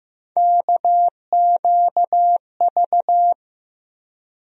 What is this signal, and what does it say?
Morse code, heard as a keyed tone, about 15 wpm, 700 Hz. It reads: KQV
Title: USA Call Sign Postfixes - 3 Letters 15wpm